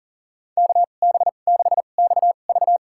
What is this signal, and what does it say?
Morse code, heard as a keyed tone, about 40 wpm, spaced as for 29 wpm, 700 Hz. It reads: KB6XV
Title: USA Call Signs - 5 Characters - 2x Character Spacing - 40wpm